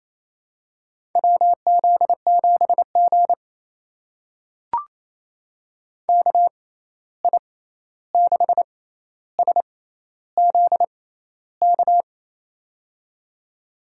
Morse code, heard as a keyed tone, about 28 wpm, spaced as for 13 wpm, 700 Hz. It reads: WZ7G KI6SZK
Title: USA Call Signs - 6x Character Spacing - 28wpm